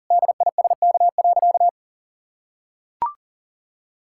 Morse code, heard as a keyed tone, about 38 wpm, 700 Hz. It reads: DISK.
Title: Sentences from Top 1000 Words - Review Part 6 - 40wpm